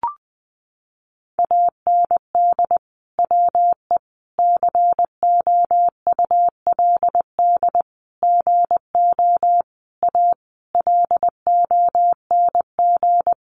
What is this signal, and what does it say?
Morse code, heard as a keyed tone, about 20 wpm, 700 Hz. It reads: AND WE COULD GO A LONG
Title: Sentences from Top 1200 Words 20wpm